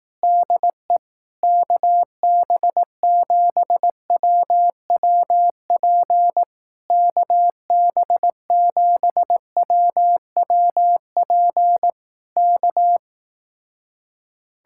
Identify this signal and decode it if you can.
Morse code, heard as a keyed tone, about 18 wpm, 700 Hz. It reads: DE KB7WWP KB7WWP K
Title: Speed-Racing Common QSO Phrases Encore - 15wpm